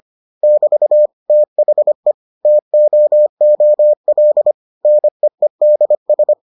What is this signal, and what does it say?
Morse code, heard as a keyed tone, about 25 wpm, 600 Hz. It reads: <BT> THE TOOL NEEDS